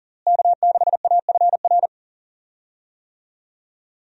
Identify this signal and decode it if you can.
Morse code, heard as a keyed tone, about 40 wpm, 700 Hz. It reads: K6AFR
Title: USA Call Signs - 5 Characters - Repeated 3x - 40wpm